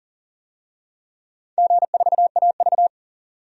Morse code, heard as a keyed tone, about 40 wpm, 700 Hz. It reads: G4AV
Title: DXCC-100 Call Signs - 4 Characters - 40wpm